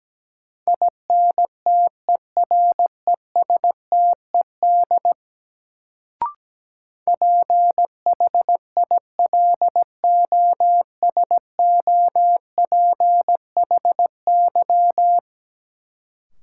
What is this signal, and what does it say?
Morse code, heard as a keyed tone, about 17 wpm, 700 Hz. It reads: INTERESTED PHILOSOPHY
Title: Sentences from Top 900 Words - New Word Drill - 17wpm